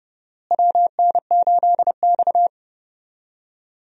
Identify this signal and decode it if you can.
Morse code, heard as a keyed tone, about 32 wpm, 700 Hz. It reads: WN8X
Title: USA Call Signs - 4 Characters - Repeated 3x - 30wpm